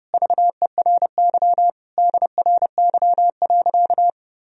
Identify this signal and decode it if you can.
Morse code, heard as a keyed tone, about 30 wpm, 700 Hz. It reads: VERY DRY.
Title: Sentences from Top 1000 Words - Review Part 4 - 30wpm